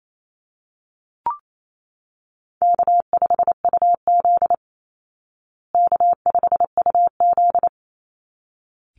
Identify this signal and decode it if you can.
Morse code, heard as a keyed tone, about 28 wpm, 700 Hz. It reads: K5UZ K5UZ
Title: Easy Call Signs Repeated 2x - 28wpm